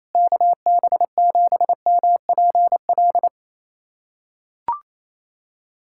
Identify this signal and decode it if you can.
Morse code, heard as a keyed tone, about 27 wpm, 700 Hz. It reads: KB7MPL
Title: USA Callsign DE Callsign - 6 Characters - 28wpm